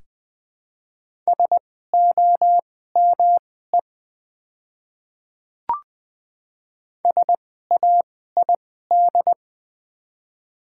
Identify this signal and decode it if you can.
Morse code, heard as a keyed tone, about 20 wpm, spaced as for 15 wpm, 700 Hz. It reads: SOME SAID
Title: 2x Character Spacing - Top 100 Words - 20wpm